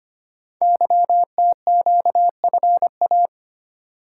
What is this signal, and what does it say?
Morse code, heard as a keyed tone, about 25 wpm, 700 Hz. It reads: YTQFA
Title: Code Groups - 5 Letters - 25wpm